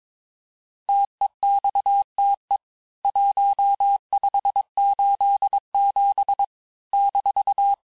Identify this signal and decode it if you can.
Morse code, heard as a keyed tone, about 22 wpm, 800 Hz. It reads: TEXTE 1587 -
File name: cours-du-lundi files / cours-1587-texte-ponctuations-vitesse-16mpm-puis-21mpm